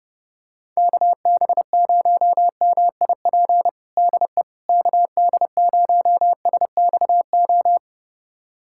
Morse code, heard as a keyed tone, about 30 wpm, 700 Hz. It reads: KB0MIP DE KD0SXO
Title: USA Callsign DE Callsign 30wpm